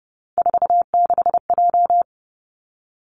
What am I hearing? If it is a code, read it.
Morse code, heard as a keyed tone, about 30 wpm, 700 Hz. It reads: 46J
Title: Three Challenging Letter-Number Combinations 30wpm